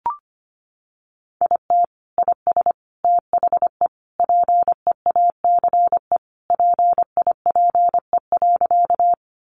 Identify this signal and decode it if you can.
Morse code, heard as a keyed tone, about 25 wpm, 700 Hz. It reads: IT IS THE PEACE PIPE.
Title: Sentences from Top 1500 Words 25wpm